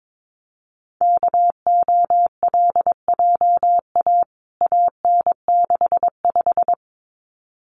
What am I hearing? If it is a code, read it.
Morse code, heard as a keyed tone, about 22 wpm, 700 Hz. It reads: KOLJA AN65
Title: CWT Exchange 22wpm